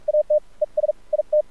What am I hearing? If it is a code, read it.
Morse code, heard as a keyed tone, about 46 wpm, spaced as for 39 wpm, 600 Hz, with radio noise. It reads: AT ES IT
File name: archived / CRSnet